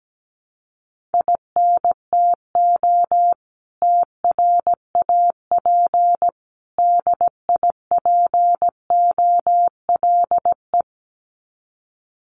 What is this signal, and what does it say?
Morse code, heard as a keyed tone, about 17 wpm, 700 Hz. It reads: INTO TRAP DIPOLE
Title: Common QSO Phrases 17wpm